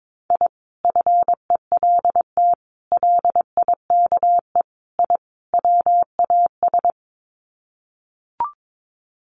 Morse code, heard as a keyed tone, about 22 wpm, 700 Hz. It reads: I FELT LIKE I WAS
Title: Sets of 5 Words Encore 22wpm